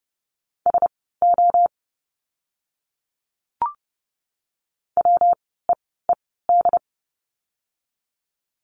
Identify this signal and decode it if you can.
Morse code, heard as a keyed tone, about 30 wpm, spaced as for 17 wpm, 700 Hz. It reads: SO WEED
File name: course / 30:10wpm / 115 Lesson 19f - Letter w - Words 30wpm